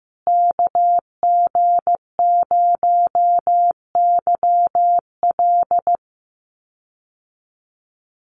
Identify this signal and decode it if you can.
Morse code, heard as a keyed tone, about 15 wpm, 700 Hz. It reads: KG0YL
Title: Call Signs Repeated 2x - 15wpm